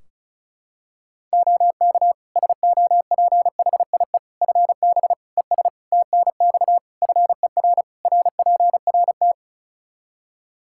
Morse code, heard as a keyed tone, about 34 wpm, 700 Hz. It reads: OK SOPHIE FB ES TNX FER RPRT
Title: Practice QSO - Exchange 2 - Info - 35wpm